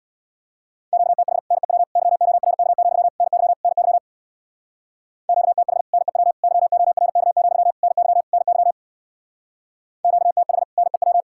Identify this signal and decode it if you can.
Morse code, heard as a keyed tone, about 66 wpm, 700 Hz. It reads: QTH NEW YORK, NY NY QTH NEW YORK, NY NY QTH NEW